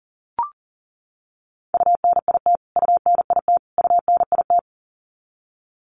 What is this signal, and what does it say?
Morse code, heard as a keyed tone, about 40 wpm, 700 Hz. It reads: UNIT UNIT UNIT
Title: Top 500 Words Repeated 40wpm